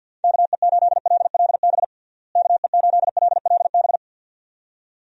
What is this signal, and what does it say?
Morse code, heard as a keyed tone, about 51 wpm, 700 Hz. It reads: KE8LLB KE8LLB
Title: USA Call Signs - 6 Characters - Repeated 2x - 50wpm